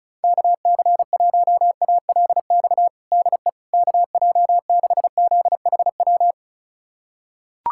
Morse code, heard as a keyed tone, about 36 wpm, 700 Hz. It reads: KC1ALX DE KJ6ZHW
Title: USA Callsign DE Callsign - 6 Characters - 35wpm